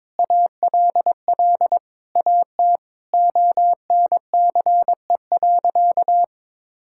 Morse code, heard as a keyed tone, about 22 wpm, 700 Hz. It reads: ALL AT ONCE.
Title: Sentences from Top 1000 Words - Review Part 5 - 22wpm